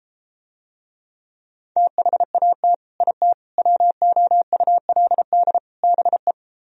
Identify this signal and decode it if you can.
Morse code, heard as a keyed tone, about 33 wpm, 700 Hz. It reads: THAT IT WOULD BE